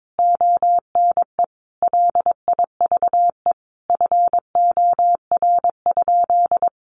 Morse code, heard as a keyed tone, about 22 wpm, 700 Hz. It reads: ONE LIVE FOR?
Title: Sentences from Top 200 Words 22wpm